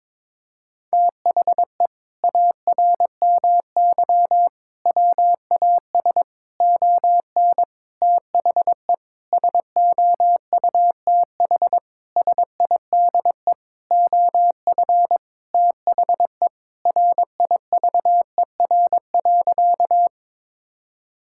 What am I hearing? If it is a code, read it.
Morse code, heard as a keyed tone, about 22 wpm, 700 Hz. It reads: THE ARMY WAS ON THE SOUTH SIDE OF THE RIVER.
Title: Sentences from Top 1000 Words - Review Part 5 - 22wpm